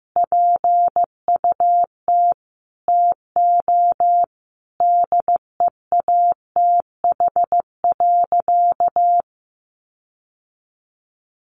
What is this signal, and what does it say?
Morse code, heard as a keyed tone, about 15 wpm, 700 Hz. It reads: PUT TO DEATH.
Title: Sentences from Top 1400 Words 15wpm